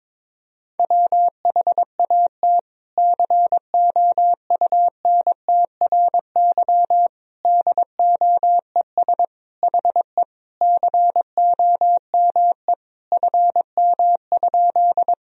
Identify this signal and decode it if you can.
Morse code, heard as a keyed tone, about 22 wpm, 700 Hz. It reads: WHAT COUNTRY DOES HE COME FM?
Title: Sentences from Top 300 Words and Abbreviations 22wpm